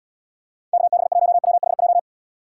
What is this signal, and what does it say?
Morse code, heard as a keyed tone, about 78 wpm, 700 Hz. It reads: KB1WHJ